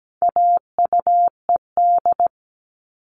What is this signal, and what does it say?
Morse code, heard as a keyed tone, about 17 wpm, 700 Hz. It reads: AUED
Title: All Cut Numbers - 4-Digit 17wpm